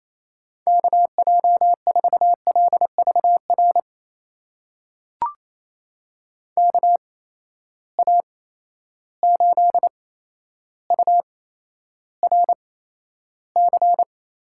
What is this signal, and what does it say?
Morse code, heard as a keyed tone, about 27 wpm, spaced as for 12 wpm, 700 Hz. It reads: KJ4LVR KA8URC
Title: USA Call Signs - 8x Character Spacing - 28wpm